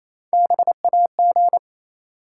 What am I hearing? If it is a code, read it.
Morse code, heard as a keyed tone, about 28 wpm, 700 Hz. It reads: BAG
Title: Three Letter Words - 28wpm